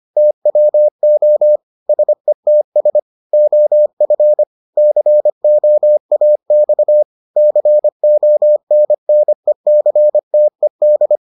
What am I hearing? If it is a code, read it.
Morse code, heard as a keyed tone, about 25 wpm, 600 Hz. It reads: TWO SETS OF COAX CONNECTED